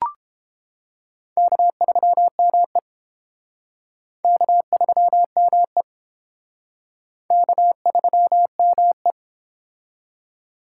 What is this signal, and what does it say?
Morse code, heard as a keyed tone, about 30 wpm, 700 Hz. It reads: K3ME K3ME K3ME